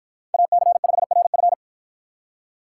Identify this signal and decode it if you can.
Morse code, heard as a keyed tone, about 53 wpm, 700 Hz. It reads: AC5RF